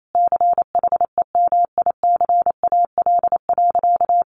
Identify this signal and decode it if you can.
Morse code, heard as a keyed tone, about 28 wpm, 700 Hz. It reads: CHEMICAL.
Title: Sentences from Top 1200 Words 28wpm